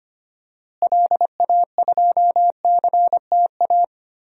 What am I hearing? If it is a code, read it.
Morse code, heard as a keyed tone, about 25 wpm, 700 Hz. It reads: LA2CTA
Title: DXCC-100 Call Signs - 6 Characters - Repeated 2x - 25wpm